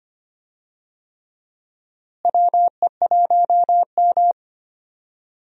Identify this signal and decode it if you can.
Morse code, heard as a keyed tone, about 25 wpm, 700 Hz. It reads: WE1M
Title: USA Call Signs - 4 Characters - Repeated 3x - 25wpm